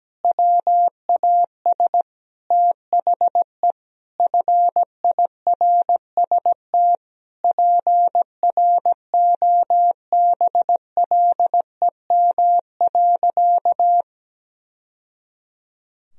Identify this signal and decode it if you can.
Morse code, heard as a keyed tone, about 17 wpm, 700 Hz. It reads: WAS THE FIRST PROBLEM.
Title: Sentences from Top 1000 Words - Review Part 2 - 17wpm